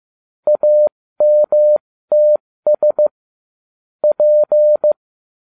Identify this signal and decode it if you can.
Morse code, heard as a keyed tone, about 15 wpm, spaced as for 13 wpm, 600 Hz. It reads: AMTS P